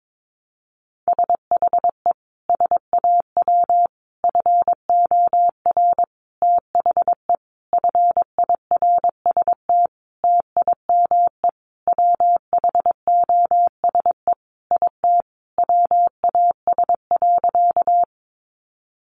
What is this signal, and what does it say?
Morse code, heard as a keyed tone, about 22 wpm, 700 Hz. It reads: SHE SAW FOR THE FIRST TIME WHOSE IT WAS.
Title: Sentences from Top 1200 Words 22wpm